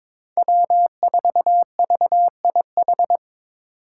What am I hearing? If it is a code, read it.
Morse code, heard as a keyed tone, about 23 wpm, 700 Hz. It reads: W4VIH
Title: Code Groups - 5 Letter:Numbers - 22wpm